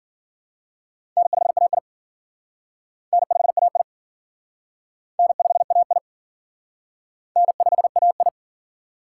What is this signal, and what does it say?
Morse code, heard as a keyed tone, about 53 wpm, 700 Hz. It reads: N5AI N5AI N5AI N5AI